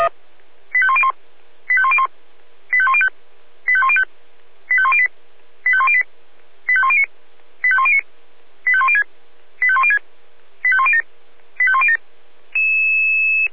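Five-tone selective call (ZVEI), heard as quick runs of five tones, tones of about 70 ms. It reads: 85101 85101 85105 85105 85108 85108 85109 85109 85106 85106 85107 85107
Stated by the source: ZVEI five-tone selective call 85106, received on SDR.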